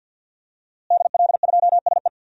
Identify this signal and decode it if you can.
Morse code, heard as a keyed tone, about 52 wpm, 700 Hz. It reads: DL2SE
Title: DXCC-100 Call Signs - 5 Characters - Repeated 2x - 50wpm